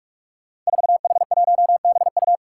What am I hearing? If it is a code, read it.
Morse code, heard as a keyed tone, about 46 wpm, 700 Hz. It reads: 4H1BU
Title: Five Challenging Letter-Number Combinations 45wpm